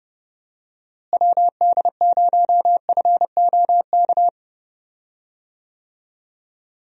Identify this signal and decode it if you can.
Morse code, heard as a keyed tone, about 31 wpm, 700 Hz. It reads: WD0FOK
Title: USA Call Signs - 6 Characters - Repeated 3x - 30wpm